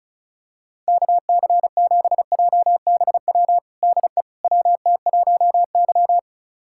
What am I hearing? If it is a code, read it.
Morse code, heard as a keyed tone, about 34 wpm, 700 Hz. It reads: KC7JBW DE WT1Y
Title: USA Callsign DE Callsign 35wpm